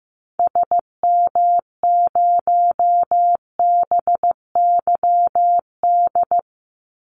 Morse code, heard as a keyed tone, about 15 wpm, 700 Hz. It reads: SM0BYD
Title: Call Signs Repeated 2x - 15wpm